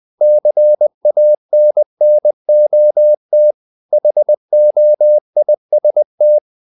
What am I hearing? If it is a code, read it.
Morse code, heard as a keyed tone, about 20 wpm, 600 Hz. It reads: CANNOT HOIST
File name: the old man and the sea / the old man and the sea part 1 20wpm